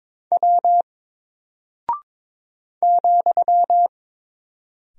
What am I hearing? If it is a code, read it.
Morse code, heard as a keyed tone, about 22 wpm, 700 Hz. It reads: W ,
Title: Single Character (Letter:Number:Punctuation:Prosign) Alt 22wpm